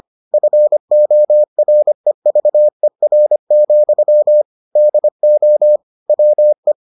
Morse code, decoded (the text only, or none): FOREVER, DO WE